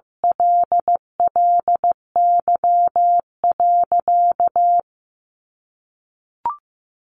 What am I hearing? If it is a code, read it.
Morse code, heard as a keyed tone, about 14 wpm, 700 Hz. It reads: LLY.
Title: Sentences from Top 1400 Words 15wpm